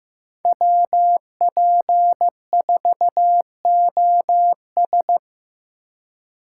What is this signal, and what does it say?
Morse code, heard as a keyed tone, about 15 wpm, 700 Hz. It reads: WP4OS
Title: USA Call Signs - 5 Characters - 2x Character Spacing - 15wpm